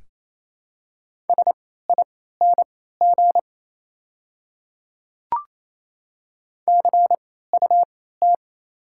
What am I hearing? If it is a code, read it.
Morse code, heard as a keyed tone, about 27 wpm, spaced as for 16 wpm, 690 Hz. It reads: SING CUT
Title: 3x Character Spacing - Top 500 Words - 28wpm